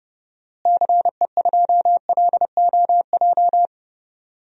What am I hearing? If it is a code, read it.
Morse code, heard as a keyed tone, about 30 wpm, 700 Hz. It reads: CE2LOJ
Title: DXCC-100 Call Signs - 6 Characters - Repeated 3x - 30wpm